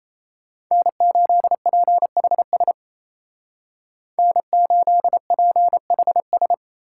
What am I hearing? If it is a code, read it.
Morse code, heard as a keyed tone, about 33 wpm, 700 Hz. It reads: N8PHS N8PHS
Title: Speed Racing - USA 1x3 Call Signs 28wpm